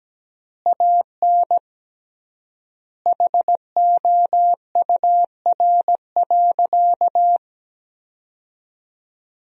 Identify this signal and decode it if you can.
Morse code, heard as a keyed tone, about 17 wpm, 700 Hz. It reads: AN HOUR.
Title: 3x Word-Spacing - Sentences from Top 400 Words - 17wpm